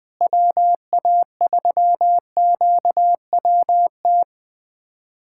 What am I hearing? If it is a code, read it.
Morse code, heard as a keyed tone, about 20 wpm, 700 Hz. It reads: WA3QWT
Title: USA Callsign DE Callsign - 6 Characters - 20wpm